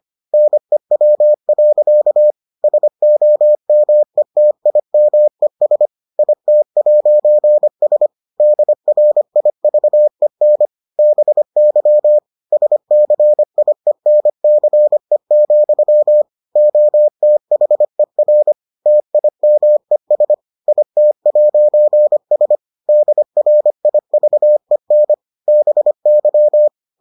Morse code, heard as a keyed tone, about 25 wpm, 600 Hz. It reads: NEW. SOMETIMES IT'S DRIVEN BY SCIENCE, OTHER TIMES IT'S DRIVEN BY